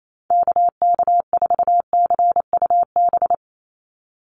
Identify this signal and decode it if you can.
Morse code, heard as a keyed tone, about 28 wpm, 700 Hz. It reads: KK4CUB
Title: USA Call Signs 28wpm